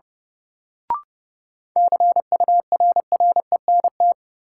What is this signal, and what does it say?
Morse code, heard as a keyed tone, about 30 wpm, 700 Hz. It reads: CURRENT
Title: Sentences from Top 1000 Words - Review Part 4 - New Word Drill - 30wpm